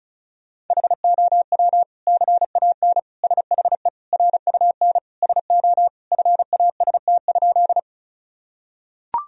Morse code, heard as a keyed tone, about 34 wpm, 700 Hz. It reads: HOW CAN SHE RUN SO FAST?
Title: Sentences from Top 1000 Words - Review Part 2 - 35wpm